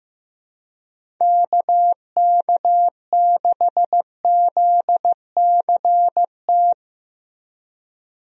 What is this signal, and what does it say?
Morse code, heard as a keyed tone, about 15 wpm, 700 Hz. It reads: KK6ZCT
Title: USA Call Signs - 6 Characters - Repeated 3x - 15wpm